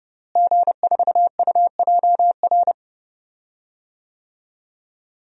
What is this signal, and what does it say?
Morse code, heard as a keyed tone, about 30 wpm, 700 Hz. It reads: G4UJR
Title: DXCC-100 Call Signs - 5 Characters - Repeated 2x - 30wpm